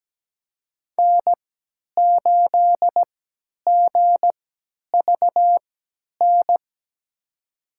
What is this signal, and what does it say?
Morse code, heard as a keyed tone, about 17 wpm, spaced as for 10 wpm, 700 Hz. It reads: N8GVN
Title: USA Call Signs - 5 Characters - 3x Character Spacing - 17wpm